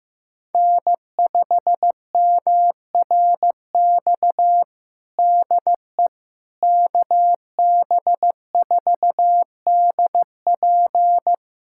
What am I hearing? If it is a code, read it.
Morse code, heard as a keyed tone, about 15 wpm, 700 Hz. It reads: N5MRX DE KB4DP
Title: USA Callsign DE Callsign - 5 Characters - 15wpm